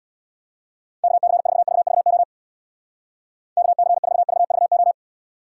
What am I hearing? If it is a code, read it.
Morse code, heard as a keyed tone, about 62 wpm, 700 Hz. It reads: KC4VVO KC4VVO
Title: Speed Racing - USA 2x3 Call Signs 50wpm